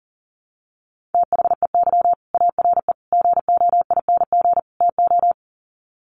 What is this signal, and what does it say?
Morse code, heard as a keyed tone, about 40 wpm, 700 Hz. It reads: THEY ARE GOING TO
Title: Sets of 4 Words 40wpm